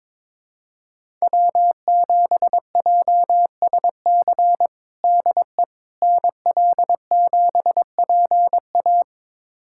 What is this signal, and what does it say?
Morse code, heard as a keyed tone, about 22 wpm, 700 Hz. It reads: W7JSC DE NL7PA
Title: USA Callsign DE Callsign - 5 Characters - 22wpm